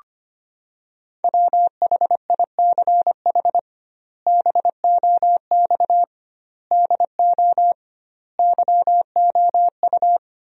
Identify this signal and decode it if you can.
Morse code, heard as a keyed tone, about 25 wpm, 700 Hz. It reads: WHICH BOX DO YOU
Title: 2x Word-Spacing - Sentences from Top 500 Words - 25wpm